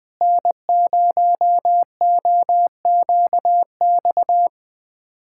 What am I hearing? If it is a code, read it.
Morse code, heard as a keyed tone, about 20 wpm, 700 Hz. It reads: N0OQX
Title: USA Callsign DE Callsign - 5 Characters - 20wpm